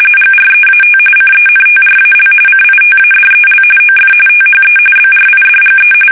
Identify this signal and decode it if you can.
RTTY, about 45.45 baud. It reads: GREAT CENTRAL CITADEL OF THE BRIDGE
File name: fsk850 / battleshipnj